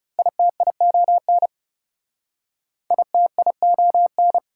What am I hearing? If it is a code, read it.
Morse code, heard as a keyed tone, about 38 wpm, 700 Hz. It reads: ITION ITION